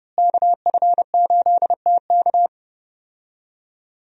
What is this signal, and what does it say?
Morse code, heard as a keyed tone, about 32 wpm, 700 Hz. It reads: KF8TK
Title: USA Call Signs - 5 Characters - Repeated 3x - 30wpm